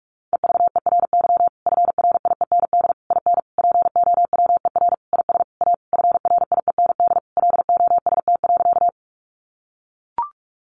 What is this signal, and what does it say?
Morse code, heard as a keyed tone, about 44 wpm, 700 Hz. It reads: EVERY FRIEND IN POWER IS A FRIEND LOST.
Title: Sentences from Top 1400 Words 45wpm